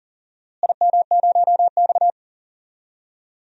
Morse code, heard as a keyed tone, about 40 wpm, 700 Hz. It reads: IM0X
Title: DXCC-100 Call Signs - 4 Characters - Repeated 2x - 40wpm